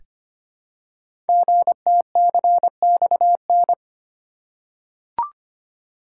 Morse code, heard as a keyed tone, about 25 wpm, 700 Hz. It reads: GTCXN
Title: Code Groups - 5 Letters - 25wpm